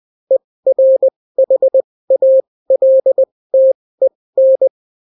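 Morse code, heard as a keyed tone, about 20 wpm, spaced as for 16 wpm, 530 Hz. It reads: ERHALTEN
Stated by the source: Daily Morse weather and top stories at 16 wpm, clean without QRN.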